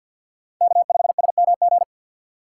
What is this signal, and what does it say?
Morse code, heard as a keyed tone, about 52 wpm, 700 Hz. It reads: K5SMG
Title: USA Call Signs - 5 Characters - 2x Character Spacing - 50wpm